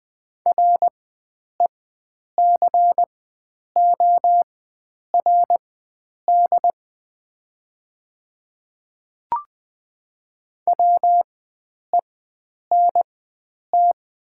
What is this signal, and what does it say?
Morse code, heard as a keyed tone, about 20 wpm, spaced as for 9 wpm, 700 Hz. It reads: RECORD WENT
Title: 4x Character Spacing - Top 500 Words - 20wpm